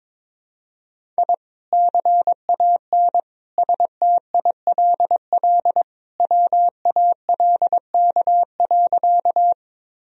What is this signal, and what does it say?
Morse code, heard as a keyed tone, about 22 wpm, 700 Hz. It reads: I CAN STILL WALK.
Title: Sentences from Top 1000 Words - Review Part 1 - 22wpm